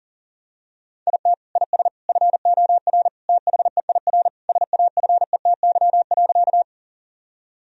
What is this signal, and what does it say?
Morse code, heard as a keyed tone, about 37 wpm, 700 Hz. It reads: IT IS FOR THEIR SAFETY.
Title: Sentences from Top 1000 Words - Review Part 4 - 40wpm